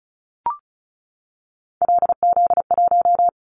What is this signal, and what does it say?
Morse code, heard as a keyed tone, about 34 wpm, 700 Hz. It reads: LZ1
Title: Callsign Prefixes for Top 100 DXCC - 35wpm